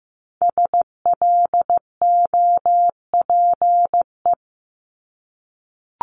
Morse code, heard as a keyed tone, about 15 wpm, 700 Hz. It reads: SLOPE
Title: Multi-letter Phonemes Repeated 3x 15wpm